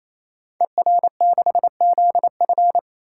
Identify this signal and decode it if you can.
Morse code, heard as a keyed tone, about 29 wpm, 700 Hz. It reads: ER6ZF
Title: Code Groups - 5 Letter:Numbers - 28wpm